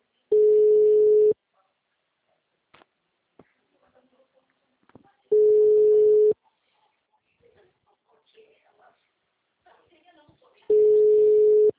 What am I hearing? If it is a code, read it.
Ringback tone (Europe, 425 Hz), with speech.